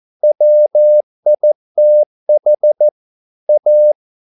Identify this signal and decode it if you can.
Morse code, heard as a keyed tone, about 14 wpm, 600 Hz. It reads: WITH A